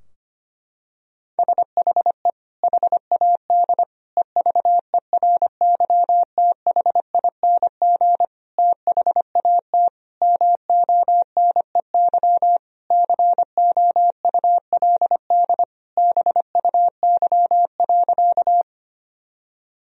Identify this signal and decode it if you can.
Morse code, heard as a keyed tone, about 25 wpm, 700 Hz. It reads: SHE HAD EVERYTHING THAT MONEY COULD BUY.
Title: Sentences from Top 1000 Words - Review Part 4 - 25wpm